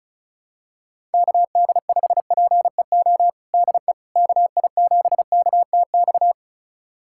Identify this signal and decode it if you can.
Morse code, heard as a keyed tone, about 36 wpm, 700 Hz. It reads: KD5PEO DE KI7KTX
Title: USA Callsign DE Callsign - 6 Characters - 35wpm